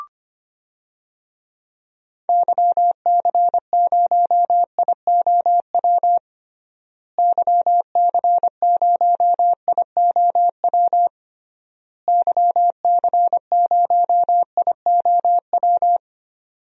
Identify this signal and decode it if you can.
Morse code, heard as a keyed tone, about 25 wpm, 700 Hz. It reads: YC0IOW YC0IOW YC0IOW
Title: DXCC-100 Call Signs - 6 Characters - Repeated 3x - 25wpm